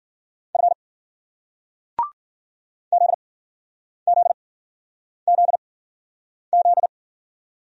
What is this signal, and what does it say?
Morse code, heard as a keyed tone, about 50 wpm, spaced as for 9 wpm, 700 Hz. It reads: F ZZZZ
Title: Speed-Racing Single Letters 40wpm